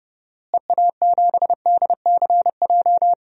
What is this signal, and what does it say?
Morse code, heard as a keyed tone, about 30 wpm, 700 Hz. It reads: EA7DCJ
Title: DXCC-100 Call Signs - 6 Characters - Repeated 2x - 30wpm